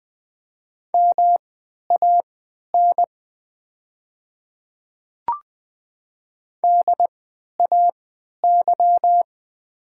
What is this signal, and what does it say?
Morse code, heard as a keyed tone, about 20 wpm, spaced as for 11 wpm, 700 Hz. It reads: MAN DAY
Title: 3x Character Spacing - Top 200 Words - 20wpm